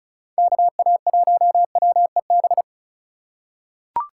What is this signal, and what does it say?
Morse code, heard as a keyed tone, about 36 wpm, 700 Hz. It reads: KA1WEB
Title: USA Call Signs - 3x Character Spacing - 35wpm